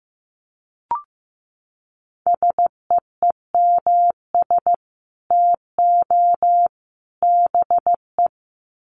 Morse code, heard as a keyed tone, about 15 wpm, 700 Hz. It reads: SEEMS TO BE
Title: Sets of 3 Words 15wpm